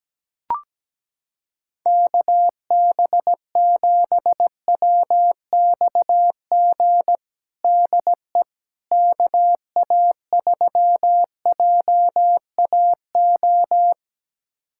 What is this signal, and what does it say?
Morse code, heard as a keyed tone, about 17 wpm, 700 Hz. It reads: KB7WXG DE KA3JAO
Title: USA Callsign DE Callsign 17wpm